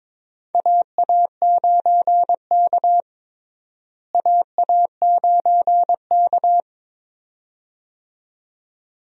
Morse code, heard as a keyed tone, about 22 wpm, 700 Hz. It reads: AA9K AA9K
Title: USA Call Signs - 4 Characters - Repeated 2x - 22wpm